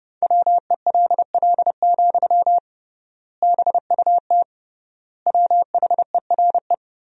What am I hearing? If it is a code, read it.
Morse code, heard as a keyed tone, about 30 wpm, 700 Hz. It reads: WELL, BUT WHERE
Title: 3x Word-Spacing - Sentences from Top 500 Words - 30wpm